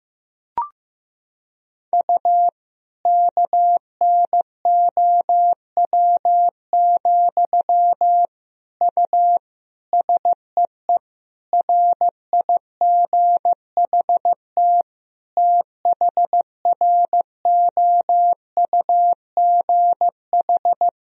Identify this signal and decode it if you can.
Morse code, heard as a keyed tone, about 15 wpm, 700 Hz. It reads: U KNOW, U SEE RIGHT THROUGH
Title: Sentences from Top 200 Words and Abbreviations 15wpm